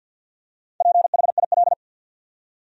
Morse code, heard as a keyed tone, about 52 wpm, 700 Hz. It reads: PHIL
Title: USA Common Names - Max 5 Letters - 50wpm